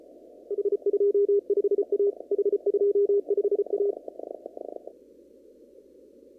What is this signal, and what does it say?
Morse code, heard as a keyed tone, about 34 wpm, 400 Hz. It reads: H25A H25A
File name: CW / kt5j wpx cw-op-k5pi